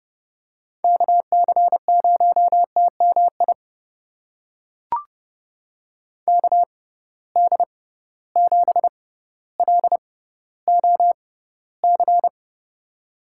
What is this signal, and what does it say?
Morse code, heard as a keyed tone, about 30 wpm, spaced as for 15 wpm, 700 Hz. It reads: KC0TMI KD7LOC
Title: USA Call Signs - 6 Characters - 6x Character Spacing - 30wpm